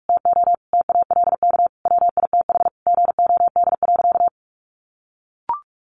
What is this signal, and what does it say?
Morse code, heard as a keyed tone, about 44 wpm, 700 Hz. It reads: TO TALK WITH GOD.
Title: A Sound Of Thunder 45wpm